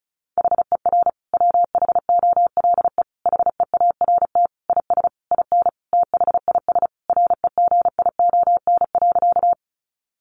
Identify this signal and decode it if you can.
Morse code, heard as a keyed tone, about 34 wpm, 700 Hz. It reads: HER WHOLE HEART IS IN THIS REGION.
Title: Sentences from Top 1500 Words 35wpm